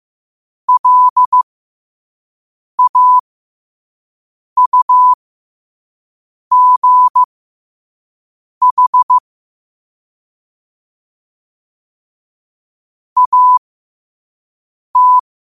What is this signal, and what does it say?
Morse code, heard as a keyed tone, about 15 wpm, spaced as for 5 wpm, 1000 Hz. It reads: LAUGH AT
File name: qotdfiles / 5wpm